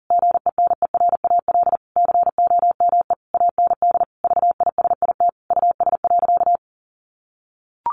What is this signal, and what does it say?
Morse code, heard as a keyed tone, about 37 wpm, 700 Hz. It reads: GENERAL COME AND VISIT US.
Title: Sentences from Top 1400 Words 40wpm